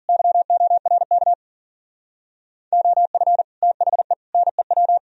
Morse code, heard as a keyed tone, about 51 wpm, 700 Hz. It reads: YORK OF THE NEW